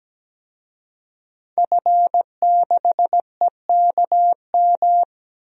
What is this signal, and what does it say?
Morse code, heard as a keyed tone, about 17 wpm, 700 Hz. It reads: F6EKM
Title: DXCC-100 Call Signs - 5 Characters - Repeated 2x - 17wpm